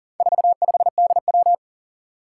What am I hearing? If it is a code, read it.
Morse code, heard as a keyed tone, about 43 wpm, 700 Hz. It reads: 45DW
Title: Four Challenging Letter-Number Combinations 40wpm